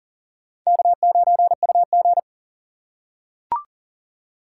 Morse code, heard as a keyed tone, about 40 wpm, 700 Hz. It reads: K9UG
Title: USA Call Signs - 4 Characters - 4x Character Spacing - 40wpm